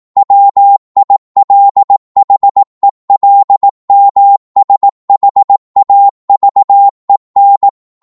Morse code, heard as a keyed tone, about 18 wpm, 800 Hz. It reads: WILHELMSHAVEN
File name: Morse / qso-18wpm